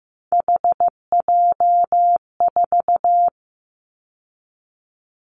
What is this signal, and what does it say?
Morse code, heard as a keyed tone, about 15 wpm, 700 Hz. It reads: HJ4
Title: Callsign Prefixes for Top 100 DXCC - 15wpm